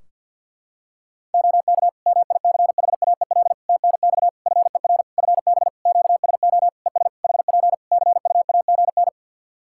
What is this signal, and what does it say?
Morse code, heard as a keyed tone, about 49 wpm, 700 Hz. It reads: OK MICHAEL TNX FER FB QSO ES HP CUAGN
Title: Practice QSO - Exchange 4 - Ending - 50wpm